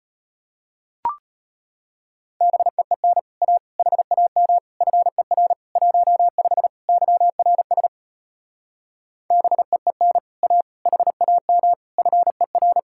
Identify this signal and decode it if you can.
Morse code, heard as a keyed tone, about 33 wpm, 700 Hz. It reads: BEEN A HAM FER 15 YRS BEEN A HAM FER